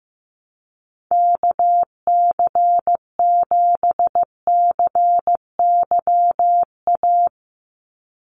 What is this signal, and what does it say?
Morse code, heard as a keyed tone, about 15 wpm, 700 Hz. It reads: KC7CYA
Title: USA Call Signs 15wpm